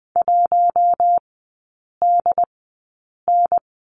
Morse code, heard as a keyed tone, about 20 wpm, spaced as for 8 wpm, 700 Hz. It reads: 1DN
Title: 067 Lesson 12e - Letter d - Three Characters 20wpm